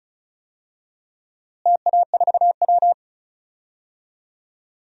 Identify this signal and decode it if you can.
Morse code, heard as a keyed tone, about 34 wpm, 700 Hz. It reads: TA4W
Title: DXCC-100 Call Signs - 4 Characters - Repeated 3x - 35wpm